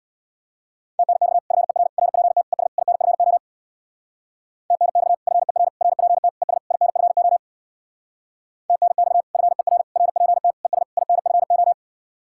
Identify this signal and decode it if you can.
Morse code, heard as a keyed tone, about 60 wpm, 700 Hz. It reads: TNX FER RPT ES INFO TNX FER RPT ES INFO TNX FER RPT ES INFO
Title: Speed-Racing Common QSO Phrases - 50wpm